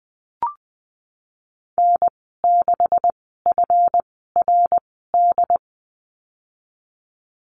Morse code, heard as a keyed tone, about 20 wpm, spaced as for 15 wpm, 700 Hz. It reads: N6FRD
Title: 140 Lesson 23g - Letter f - Call Signs 20wpm